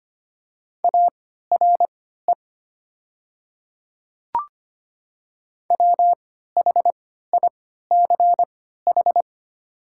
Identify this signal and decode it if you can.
Morse code, heard as a keyed tone, about 25 wpm, spaced as for 14 wpm, 700 Hz. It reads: ARE WHICH
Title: 3x Character Spacing - Top 300 Words - 25wpm